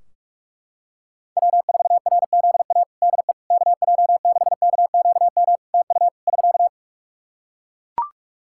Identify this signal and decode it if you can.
Morse code, heard as a keyed tone, about 44 wpm, 700 Hz. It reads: W4RZA DE KJ6KQM TU <SK>
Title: Practice QSO - Exchange 4 - Ending - 45wpm